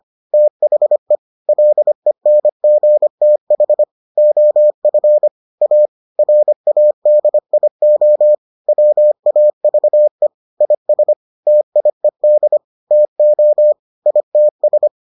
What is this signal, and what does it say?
Morse code, heard as a keyed tone, about 25 wpm, 600 Hz. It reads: THE LENGTH OF A RADIO WAVE IS TIED TO ITS